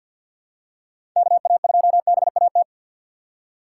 Morse code, heard as a keyed tone, about 51 wpm, 700 Hz. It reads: KA2BAT
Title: USA Call Signs - 6 Characters - 3x Character Spacing - 50wpm